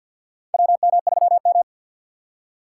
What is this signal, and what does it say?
Morse code, heard as a keyed tone, about 48 wpm, 700 Hz. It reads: WM3M
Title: USA Call Signs - 2x Character Spacing - 50wpm